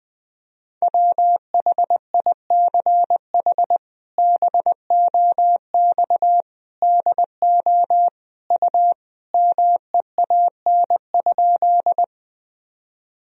Morse code, heard as a keyed tone, about 21 wpm, 700 Hz. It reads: WHICH BOX DO U MEAN?
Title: Sentences from Top 500 Words and Abbreviations 20wpm